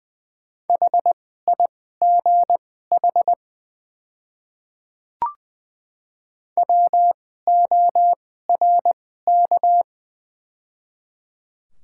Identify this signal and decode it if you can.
Morse code, heard as a keyed tone, about 21 wpm, spaced as for 15 wpm, 700 Hz. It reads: HIGH WORK
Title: 2x Character Spacing - Top 200 Words - 20wpm